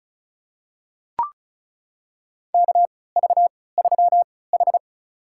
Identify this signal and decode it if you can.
Morse code, heard as a keyed tone, about 36 wpm, spaced as for 20 wpm, 700 Hz. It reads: KV3H
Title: USA Call Signs - 3x Character Spacing - 35wpm